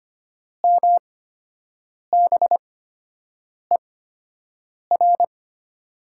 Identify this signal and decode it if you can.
Morse code, heard as a keyed tone, about 25 wpm, spaced as for 7 wpm, 700 Hz. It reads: MBER
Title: 8x Character Spacing - Top 100 Words - 25wpm